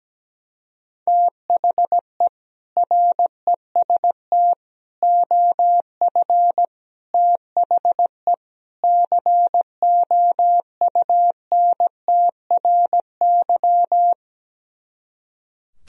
Morse code, heard as a keyed tone, about 17 wpm, 700 Hz. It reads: THE REST OF THE COUNTRY
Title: Speed-Racing - Sets of 5 Words 17wpm